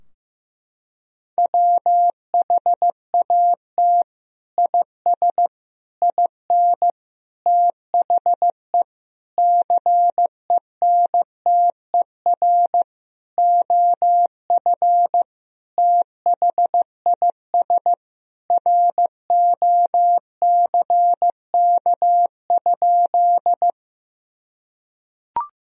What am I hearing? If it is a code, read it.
Morse code, heard as a keyed tone, about 16 wpm, 700 Hz. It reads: WHAT IS IN THE CENTER OF THIS ROCK?
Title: Sentences from Top 1000 Words - Review Part 2 - 15wpm